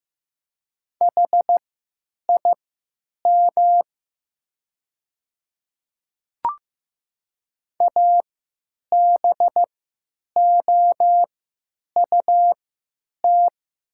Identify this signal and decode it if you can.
Morse code, heard as a keyed tone, about 15 wpm, spaced as for 9 wpm, 700 Hz. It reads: HIM ABOUT
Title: 3x Character Spacing - Top 100 Words - 15wpm